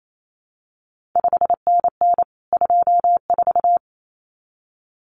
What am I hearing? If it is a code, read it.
Morse code, heard as a keyed tone, about 28 wpm, 700 Hz. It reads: 5NN 24
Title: CQ WW DX Contest Exchange - 28wpm